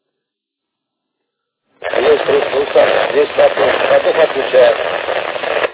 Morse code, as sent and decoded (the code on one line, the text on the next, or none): none